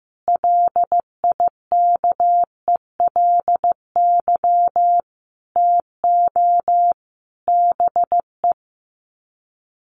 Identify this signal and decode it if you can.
Morse code, heard as a keyed tone, about 15 wpm, 700 Hz. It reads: LIKELY TO BE
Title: Sets of 5 Words 15wpm